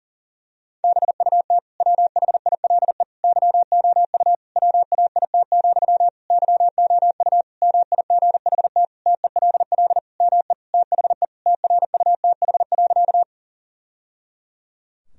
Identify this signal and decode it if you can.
Morse code, heard as a keyed tone, about 37 wpm, 700 Hz. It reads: BUT WHILE YOU WAIT, YOU MIGHT TELL ME THE TRUTH.
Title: Sentences from Top 1000 Words - Review Part 1 - 40wpm